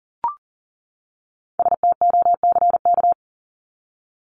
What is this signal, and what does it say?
Morse code, heard as a keyed tone, about 40 wpm, 700 Hz. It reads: STOCK
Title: Top 501-750 Words 40wpm